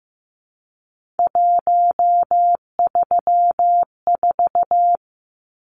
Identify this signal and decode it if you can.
Morse code, heard as a keyed tone, about 15 wpm, 700 Hz. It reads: 134
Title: Numbers - 3-Digit 15wpm